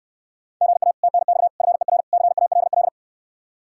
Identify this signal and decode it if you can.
Morse code, heard as a keyed tone, about 68 wpm, 700 Hz. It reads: GA TNX FER CALL